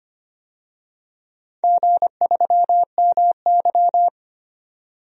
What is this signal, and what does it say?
Morse code, heard as a keyed tone, about 25 wpm, 700 Hz. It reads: G3MY
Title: DXCC-100 Call Signs - 4 Characters - 25wpm